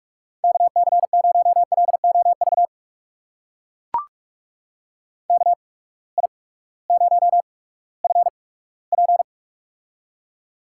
Morse code, heard as a keyed tone, about 44 wpm, spaced as for 20 wpm, 700 Hz. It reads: KC0LOV KI0FP
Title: USA Call Signs - 8x Character Spacing - 45wpm